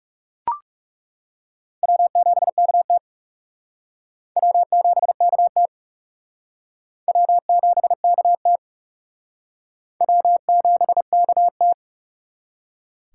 Morse code, heard as a keyed tone, about 37 wpm, 700 Hz. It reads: W7KT W7KT W7KT W7KT
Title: Speed Racing - USA 1x2 Call Signs 30wpm